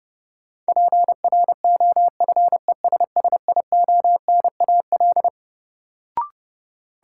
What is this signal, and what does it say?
Morse code, heard as a keyed tone, about 30 wpm, 700 Hz. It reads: PROFESSIONAL
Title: Sentences from Top 1000 Words - Review Part 4 - New Word Drill - 30wpm